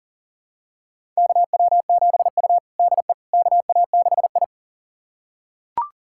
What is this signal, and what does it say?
Morse code, heard as a keyed tone, about 40 wpm, 700 Hz. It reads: KW7U DE KA6I
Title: USA Callsign DE Callsign - 4 Characters - 40wpm